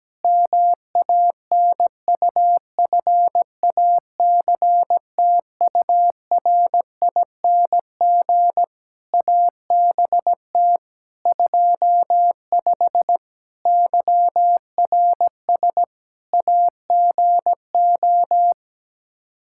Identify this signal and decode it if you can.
Morse code, heard as a keyed tone, about 17 wpm, 700 Hz. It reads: MANUFACTURING ABT 25 YRS AGO
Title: K1USN SST QSO Practice 17wpm